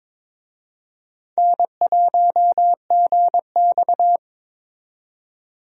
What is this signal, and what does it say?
Morse code, heard as a keyed tone, about 22 wpm, 700 Hz. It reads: N1GX
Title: USA Call Signs - 4 Characters - 22wpm